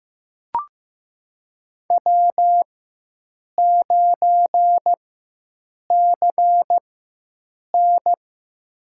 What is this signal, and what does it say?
Morse code, heard as a keyed tone, about 15 wpm, spaced as for 7 wpm, 700 Hz. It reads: W9CN
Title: USA Call Signs - 4 Characters - 4x Character Spacing - 15wpm